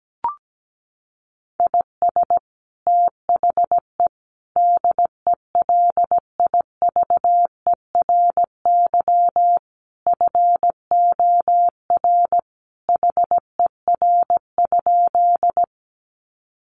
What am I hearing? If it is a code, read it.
Morse code, heard as a keyed tone, about 17 wpm, 700 Hz. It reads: IS THE DELIVERY FOR HER?
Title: Sentences from Top 1500 Words 17wpm